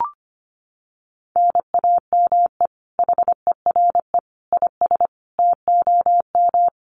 Morse code, heard as a keyed tone, about 25 wpm, 700 Hz. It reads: NAME HERE IS TOM
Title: Common QSO Phrases 25wpm